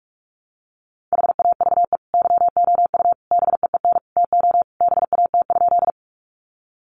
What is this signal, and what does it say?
Morse code, heard as a keyed tone, about 44 wpm, 700 Hz. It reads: HAVE YOU BEEN TO BAT?
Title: Sentences from Top 1500 Words 45wpm